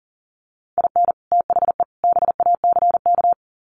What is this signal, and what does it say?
Morse code, heard as a keyed tone, about 40 wpm, 700 Hz. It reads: IN THE BACK
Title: Sets of 3 Words 40wpm